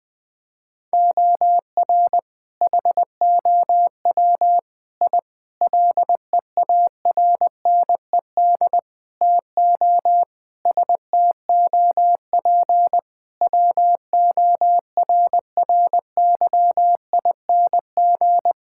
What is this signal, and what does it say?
Morse code, heard as a keyed tone, about 20 wpm, 700 Hz. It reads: OR HOW I LEARNED TO STOP WORRYING
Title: Sets of 8 Words - 20wpm